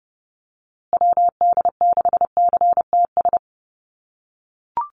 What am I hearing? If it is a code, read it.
Morse code, heard as a keyed tone, about 30 wpm, 700 Hz. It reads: WD6CTS
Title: USA Call Signs Repeated 3x 30wpm